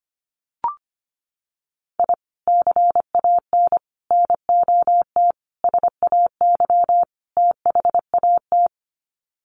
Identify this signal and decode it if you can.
Morse code, heard as a keyed tone, about 25 wpm, 700 Hz. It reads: I CAN NOT SAY THAT
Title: Sets of 5 Words Encore2 25wpm